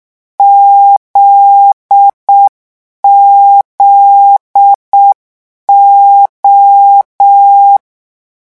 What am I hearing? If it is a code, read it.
Morse code, heard as a keyed tone, about 6 wpm, 800 Hz. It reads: ZZO
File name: morse mp3 / lecon9-vitesse 6